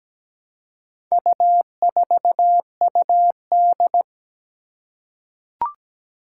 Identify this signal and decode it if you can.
Morse code, heard as a keyed tone, about 17 wpm, 700 Hz. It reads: U4UD
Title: Four Challenging Letter-Number Combinations 17wpm